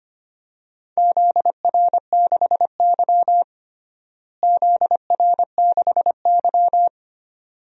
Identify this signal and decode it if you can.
Morse code, heard as a keyed tone, about 25 wpm, 700 Hz. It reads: ZR6Y ZR6Y
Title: DXCC-100 Call Signs - 4 Characters - Repeated 2x - 25wpm